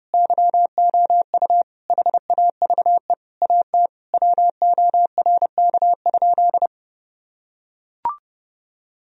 Morse code, heard as a keyed tone, about 30 wpm, 700 Hz. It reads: YOU HAVE AT WORK?
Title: Sentences from Top 1000 Words - Review Part 4 - 30wpm